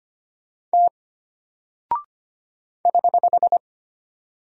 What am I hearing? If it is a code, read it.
Morse code, heard as a keyed tone, about 25 wpm, 700 Hz. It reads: T <HH>
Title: Single Character (Letter:Number:Punctuation:Prosign) Alt 25wpm